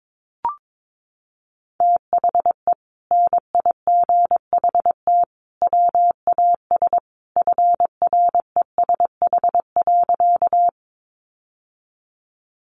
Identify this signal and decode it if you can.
Morse code, heard as a keyed tone, about 22 wpm, 700 Hz. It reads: THE NIGHT WAS FRESH.
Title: Sentences from Top 1200 Words 22wpm